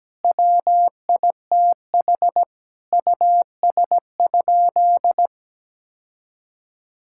Morse code, heard as a keyed tone, about 17 wpm, 700 Hz. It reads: WITH US?
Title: Sentences from Top 1000 Words - Review Part 6 - 17wpm